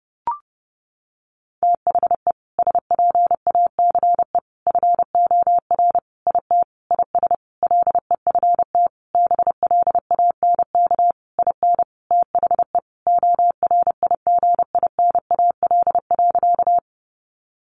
Morse code, heard as a keyed tone, about 30 wpm, 700 Hz. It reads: THE SPACE FOR IT IS LEFT BLANK IN THE ORIGINAL.
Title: Sentences from Top 1500 Words 30wpm